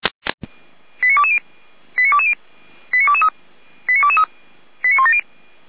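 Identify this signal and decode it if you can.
Five-tone selective call (ZVEI), heard as quick runs of five tones, tones of about 70 ms. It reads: 89220 89220 89223 89223 89170
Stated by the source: ZVEI five-tone call 89163 received by SDR.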